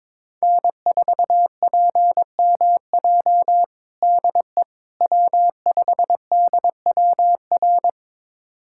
Morse code, heard as a keyed tone, about 22 wpm, 700 Hz. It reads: N4PMJ DE W5DWR
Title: USA Callsign DE Callsign - 5 Characters - 22wpm